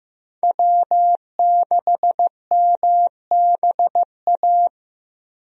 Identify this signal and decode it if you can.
Morse code, heard as a keyed tone, about 16 wpm, 700 Hz. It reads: W6MBA
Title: USA Call Signs - 5 Characters - Repeated 2x - 15wpm